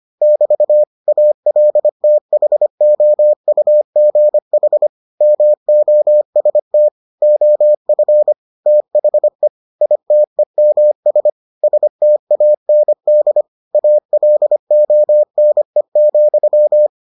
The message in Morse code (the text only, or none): <BT> ALTHOUGH MOST OF THE ITEMS STAND ALONE,